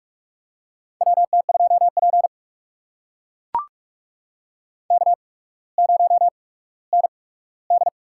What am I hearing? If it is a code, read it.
Morse code, heard as a keyed tone, about 44 wpm, spaced as for 19 wpm, 700 Hz. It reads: WT2P K0ND
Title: USA Call Signs - 4 Characters - 8x Character Spacing - 45wpm